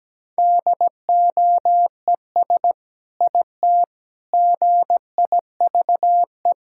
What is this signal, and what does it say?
Morse code, heard as a keyed tone, about 17 wpm, 700 Hz. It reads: DOES IT GIVE
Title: Sentences from Top 1600 Words 17wpm